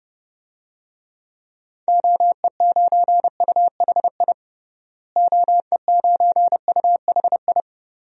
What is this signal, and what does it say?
Morse code, heard as a keyed tone, about 32 wpm, 700 Hz. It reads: OE9UHI OE9UHI
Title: DXCC-100 Call Signs - 6 Characters - Repeated 2x - 30wpm